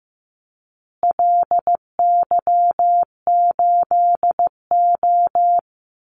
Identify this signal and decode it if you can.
Morse code, heard as a keyed tone, about 15 wpm, 700 Hz. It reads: LY8O
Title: Call Signs Repeated 2x - 15wpm